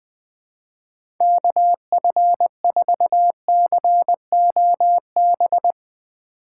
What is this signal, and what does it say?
Morse code, heard as a keyed tone, about 20 wpm, 700 Hz. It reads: KF4COB
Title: USA Call Signs - 4x Character Spacing - 20wpm